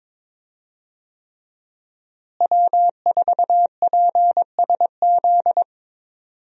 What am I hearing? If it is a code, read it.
Morse code, heard as a keyed tone, about 22 wpm, 700 Hz. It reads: W4PSZ
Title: USA Call Signs - 5 Characters - Repeated 3x - 22wpm